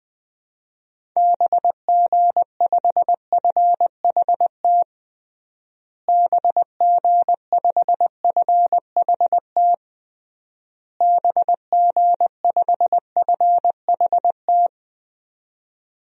Morse code, heard as a keyed tone, about 20 wpm, 700 Hz. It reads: BG5FHT BG5FHT BG5FHT
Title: DXCC-100 Call Signs - 6 Characters - Repeated 3x - 20wpm